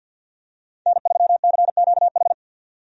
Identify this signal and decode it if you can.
Morse code, heard as a keyed tone, about 51 wpm, 700 Hz. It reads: N3CQH
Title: USA Call Signs - 5 Characters - 50wpm